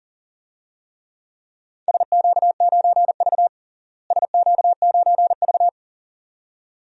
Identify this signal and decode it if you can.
Morse code, heard as a keyed tone, about 40 wpm, 700 Hz. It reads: SQ9V SQ9V
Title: DXCC-100 Call Signs - 4 Characters - Repeated 2x - 40wpm